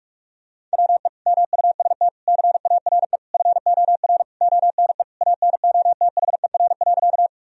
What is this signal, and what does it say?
Morse code, heard as a keyed tone, about 46 wpm, 700 Hz. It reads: WE MUST CARE FOR ONE ANOTHER.
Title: Sentences from Top 1000 Words - Review Part 1 - 45wpm